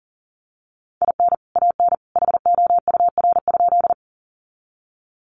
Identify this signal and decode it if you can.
Morse code, heard as a keyed tone, about 40 wpm, 700 Hz. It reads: IN AN HOUR?
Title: Sentences from Top 400 Words 40wpm